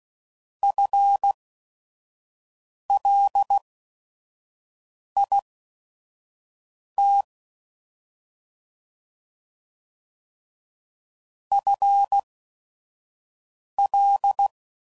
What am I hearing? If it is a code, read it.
Morse code, heard as a keyed tone, about 16 wpm, spaced as for 5 wpm, 800 Hz. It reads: FLIT FL
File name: Prefixes / FLs & THs / FL Prefixes 3